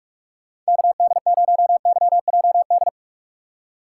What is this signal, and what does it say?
Morse code, heard as a keyed tone, about 46 wpm, 700 Hz. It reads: KD0YJD
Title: USA Call Signs - 6x Character Spacing - 45wpm